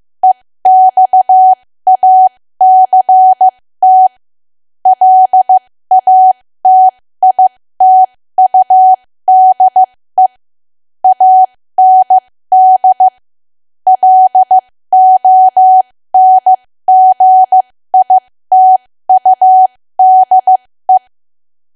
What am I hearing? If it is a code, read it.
Morse code, heard as a keyed tone, about 15 wpm, spaced as for 13 wpm, 750 Hz. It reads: EXACT LATITUDE AND LONGITUDE
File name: cwa / W1AW-Level2 / 13WPM